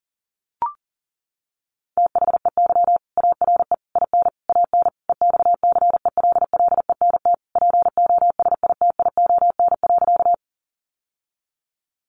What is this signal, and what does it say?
Morse code, heard as a keyed tone, about 40 wpm, 700 Hz. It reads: THEY ARE IN AN EXCELLENT POSITION.
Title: Sentences from Top 1000 Words 40wpm